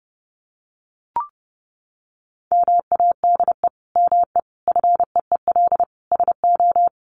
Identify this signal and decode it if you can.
Morse code, heard as a keyed tone, about 30 wpm, 700 Hz. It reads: MADE ME FEEL SO
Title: Sentences from Top 400 Words 30wpm